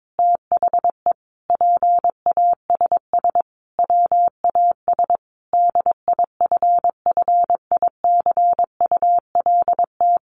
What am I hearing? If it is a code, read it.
Morse code, heard as a keyed tone, about 22 wpm, 700 Hz. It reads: THE PASS WAS DIFFICULT
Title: Sentences from Top 1200 Words 22wpm